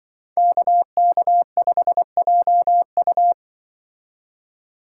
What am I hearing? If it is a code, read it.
Morse code, heard as a keyed tone, about 24 wpm, 700 Hz. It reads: KK5JU
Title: Speed Racing - USA 2x2 Call Signs 20wpm